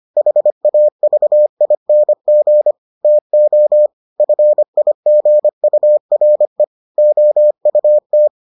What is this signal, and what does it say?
Morse code, heard as a keyed tone, about 25 wpm, 600 Hz. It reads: HAVING TO FIGURE OUT